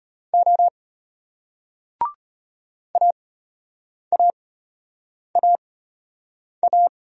Morse code, heard as a keyed tone, about 34 wpm, spaced as for 8 wpm, 700 Hz. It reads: O AAAA